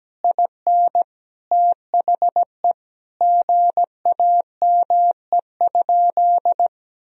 Morse code, heard as a keyed tone, about 17 wpm, 700 Hz. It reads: IN THE GAME?
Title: Sentences from Top 1000 Words - Review Part 4 - 17wpm